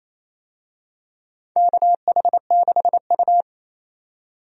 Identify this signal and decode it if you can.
Morse code, heard as a keyed tone, about 28 wpm, 700 Hz. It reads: KH6U
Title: DXCC-100 Call Signs - 4 Characters - 28wpm